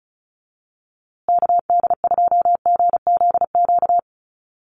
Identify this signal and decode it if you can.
Morse code, heard as a keyed tone, about 34 wpm, 700 Hz. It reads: KD2GZQ
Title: USA Call Signs Repeated 3x 35wpm